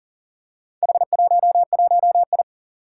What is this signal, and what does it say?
Morse code, heard as a keyed tone, about 41 wpm, 700 Hz. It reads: H11I
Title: Four Challenging Letter-Number Combinations 40wpm